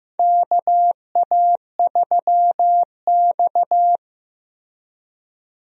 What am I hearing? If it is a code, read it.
Morse code, heard as a keyed tone, about 15 wpm, 700 Hz. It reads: KA3X
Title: USA Call Signs - 4 Characters - Repeated 3x - 15wpm